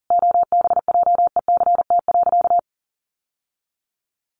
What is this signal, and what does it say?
Morse code, heard as a keyed tone, about 40 wpm, 700 Hz. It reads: OBJECT.
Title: Sentences from Top 1300 Words 40wpm